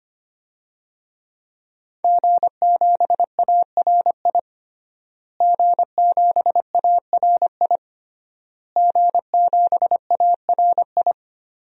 Morse code, heard as a keyed tone, about 25 wpm, 700 Hz. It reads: G7ARI G7ARI G7ARI
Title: DXCC-100 Call Signs - 5 Characters - Repeated 3x - 25wpm